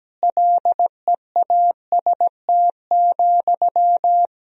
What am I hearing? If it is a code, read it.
Morse code, heard as a keyed tone, about 17 wpm, 700 Hz. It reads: LEAST,
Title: Sentences from Top 1000 Words - Review Part 4 - 17wpm